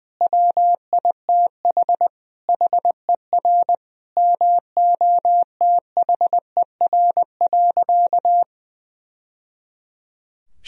Morse code, heard as a keyed tone, about 21 wpm, 700 Hz. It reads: WITH HER MOTHER.
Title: Sentences from Top 1000 Words - Review Part 4 - 20wpm